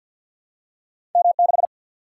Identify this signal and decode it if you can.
Morse code, heard as a keyed tone, about 48 wpm, 700 Hz. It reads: M6
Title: Single-Letter Single-Number 50wpm